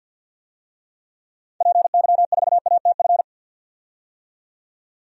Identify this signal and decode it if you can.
Morse code, heard as a keyed tone, about 52 wpm, 700 Hz. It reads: PY4ATF
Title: DXCC-100 Call Signs - 6 Characters - 50wpm